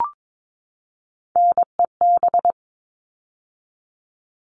Morse code, heard as a keyed tone, about 22 wpm, 700 Hz. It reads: NEB
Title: All Cut Numbers - 3-Digit 22wpm